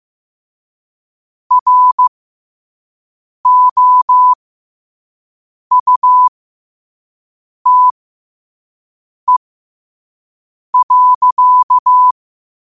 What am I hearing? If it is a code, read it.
Morse code, heard as a keyed tone, about 15 wpm, spaced as for 5 wpm, 1000 Hz. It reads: ROUTE.
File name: qotdfiles / 5wpm